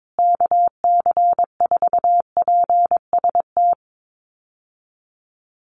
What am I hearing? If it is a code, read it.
Morse code, heard as a keyed tone, about 22 wpm, 700 Hz. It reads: KC4PST
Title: USA Call Signs Repeated 3x 22wpm